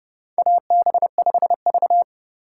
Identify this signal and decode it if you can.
Morse code, heard as a keyed tone, about 30 wpm, 700 Hz. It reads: AB5V
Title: USA Call Signs - 4 Characters - 2x Character Spacing - 30wpm